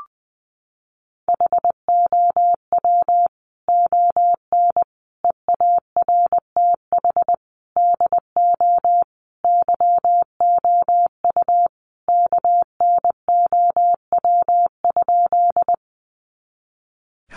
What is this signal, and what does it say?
Morse code, heard as a keyed tone, about 20 wpm, 700 Hz. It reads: HOW ON EARTH DO YOU KNOW?
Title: Sentences from Top 200 Words 20wpm